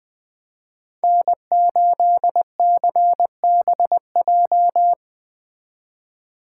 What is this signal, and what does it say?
Morse code, heard as a keyed tone, about 20 wpm, 700 Hz. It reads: N8CBJ
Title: Speed Racing - USA 1x3 Call Signs 17wpm